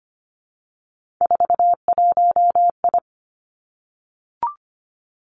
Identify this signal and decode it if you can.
Morse code, heard as a keyed tone, about 25 wpm, 700 Hz. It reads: 41I
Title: Three Challenging Letter-Number Combinations 25wpm